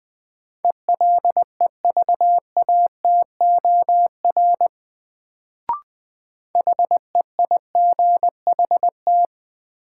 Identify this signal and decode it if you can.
Morse code, heard as a keyed tone, about 20 wpm, 700 Hz. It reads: ELEVATOR HEIGHT
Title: Sentences from Top 1600 Words - New Word Drill Alt - 20wpm